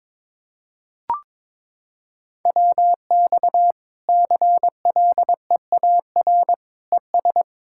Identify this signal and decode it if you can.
Morse code, heard as a keyed tone, about 22 wpm, 700 Hz. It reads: WX CLEAR ES
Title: K1USN SST QSO Practice 22wpm